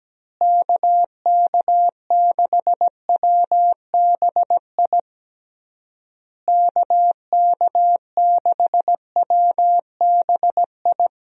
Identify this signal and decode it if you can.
Morse code, heard as a keyed tone, about 17 wpm, 700 Hz. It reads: KK6WBI KK6WBI
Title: USA Call Signs - 6 Characters - Repeated 3x - 17wpm